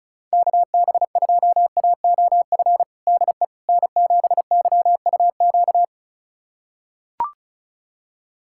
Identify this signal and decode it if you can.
Morse code, heard as a keyed tone, about 36 wpm, 690 Hz. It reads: KB2AOF DE N7YUQ
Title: USA Callsign DE Callsign 35wpm